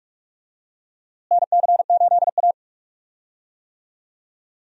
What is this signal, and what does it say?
Morse code, heard as a keyed tone, about 45 wpm, 700 Hz. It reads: NC8A
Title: USA Call Signs - 4 Characters - Repeated 3x - 45wpm